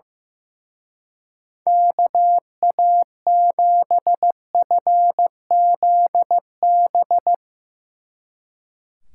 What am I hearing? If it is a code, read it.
Morse code, heard as a keyed tone, about 16 wpm, 700 Hz. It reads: KA7FZB
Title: Speed Racing - USA Call Signs 15wpm